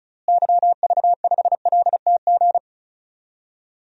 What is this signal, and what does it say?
Morse code, heard as a keyed tone, about 36 wpm, 700 Hz. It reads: YV5LTG
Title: DXCC-100 Call Signs - 6 Characters - Repeated 3x - 35wpm